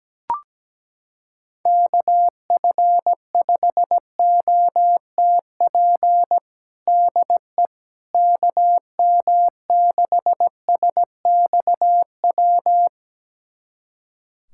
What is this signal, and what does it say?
Morse code, heard as a keyed tone, about 17 wpm, 700 Hz. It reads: KF5OTP DE KM6SXW
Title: USA Callsign DE Callsign 17wpm